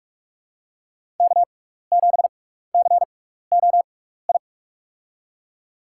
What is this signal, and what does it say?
Morse code, heard as a keyed tone, about 46 wpm, spaced as for 16 wpm, 700 Hz. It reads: K7COI
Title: USA Call Signs - 5 Characters - 6x Character Spacing - 45wpm